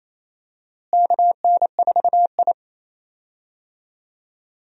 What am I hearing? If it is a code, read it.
Morse code, heard as a keyed tone, about 28 wpm, 700 Hz. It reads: KN4I
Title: USA Call Signs - 4 Characters - Repeated 3x - 28wpm